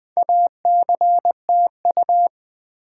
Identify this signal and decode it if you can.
Morse code, heard as a keyed tone, about 20 wpm, 700 Hz. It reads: ACTU
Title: Build A Word - 2 Letters At A Time - 8-Letter Words - 20wpm